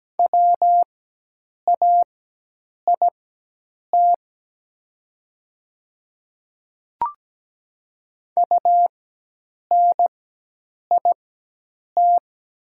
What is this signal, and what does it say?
Morse code, heard as a keyed tone, about 17 wpm, spaced as for 8 wpm, 700 Hz. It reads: WAIT UNIT
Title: 4x Character Spacing - Top 500 Words - 17wpm